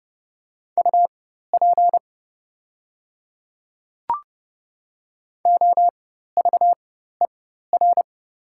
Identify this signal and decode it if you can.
Morse code, heard as a keyed tone, about 30 wpm, spaced as for 14 wpm, 700 Hz. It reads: UP OVER
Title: 4x Character Spacing - Top 100 Words - 30wpm